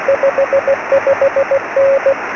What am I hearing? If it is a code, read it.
Morse code, heard as a keyed tone, about 16 wpm, 550 Hz, with radio noise. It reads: 55N
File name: sm7lcb / g3xdy 23cm